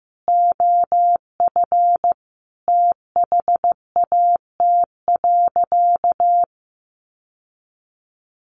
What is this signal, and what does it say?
Morse code, heard as a keyed tone, about 15 wpm, 700 Hz. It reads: OF THAT.
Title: Sentences from Top 1200 Words 15wpm